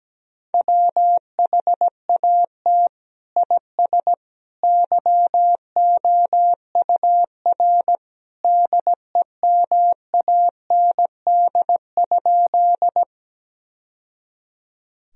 Morse code, heard as a keyed tone, about 17 wpm, 700 Hz. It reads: WHAT IS YOUR DEMAND?
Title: Sentences from Top 1000 Words - Review Part 4 - 17wpm